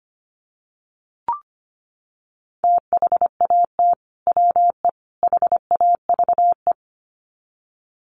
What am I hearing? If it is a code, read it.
Morse code, heard as a keyed tone, about 25 wpm, 700 Hz. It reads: THAT WE HAVE
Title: Sets of 3 Words 25wpm